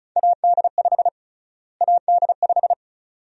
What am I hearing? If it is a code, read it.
Morse code, heard as a keyed tone, about 36 wpm, 700 Hz. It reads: AD5 AD5
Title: USA Call Sign Prefixes - Repeated 2x - 35wpm